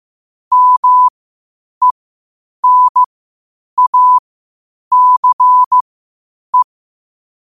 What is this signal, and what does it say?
Morse code, heard as a keyed tone, about 15 wpm, spaced as for 8 wpm, 1000 Hz. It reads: MENACE
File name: qotdfiles / 8wpm